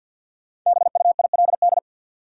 Morse code, heard as a keyed tone, about 47 wpm, 700 Hz. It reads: BUILD
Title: Sentences from Top 1100 Words - New Word Drill Alt - 50wpm